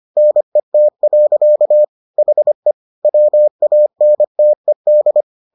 Morse code, heard as a keyed tone, about 25 wpm, 600 Hz. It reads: NET. HE WANTED